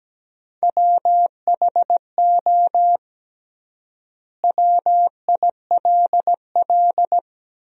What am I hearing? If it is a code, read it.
Morse code, heard as a keyed tone, about 17 wpm, 700 Hz. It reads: WHO WILL
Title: 3x Word-Spacing - Sentences from Top 100 Words - 17wpm